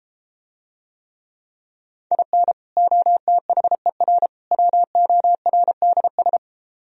Morse code, heard as a keyed tone, about 33 wpm, 700 Hz. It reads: IN OTHER WORDS